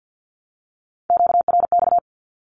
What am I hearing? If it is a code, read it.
Morse code, heard as a keyed tone, about 50 wpm, 700 Hz. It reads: QRX
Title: Q-Codes 50wpm